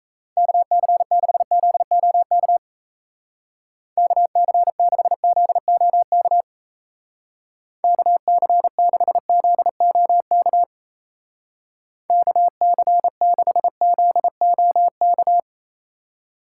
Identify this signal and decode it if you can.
Morse code, heard as a keyed tone, about 35 wpm, 700 Hz. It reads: KC6ZOK KC6ZOK KC6ZOK KC6ZOK
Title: Speed Racing - USA 2x3 Call Signs 28wpm